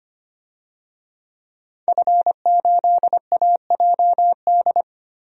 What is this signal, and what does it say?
Morse code, heard as a keyed tone, about 25 wpm, 700 Hz. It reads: F8AJD
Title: DXCC-100 Call Signs - 5 Characters - 25wpm